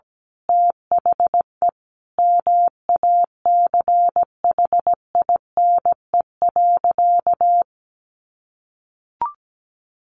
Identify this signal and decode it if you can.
Morse code, heard as a keyed tone, about 17 wpm, 700 Hz. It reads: THE MACHINE.
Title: A Sound Of Thunder 17wpm